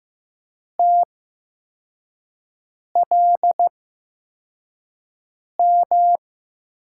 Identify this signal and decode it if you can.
Morse code, heard as a keyed tone, about 15 wpm, spaced as for 4 wpm, 700 Hz. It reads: TLM
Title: USA Call Signs - 8x Character Spacing - 15wpm